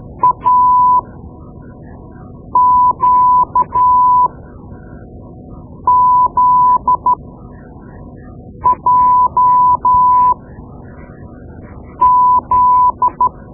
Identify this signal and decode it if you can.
Morse code, heard as a keyed tone, about 13 wpm, spaced as for 5 wpm, 1000 Hz. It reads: AQZJZ
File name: Morse Lessons / Morse Lessons / Tape-2B